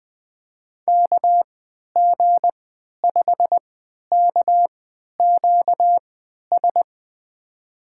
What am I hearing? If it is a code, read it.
Morse code, heard as a keyed tone, about 21 wpm, spaced as for 11 wpm, 700 Hz. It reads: KG5KQS
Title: USA Call Signs - 6 Characters - 3x Character Spacing - 20wpm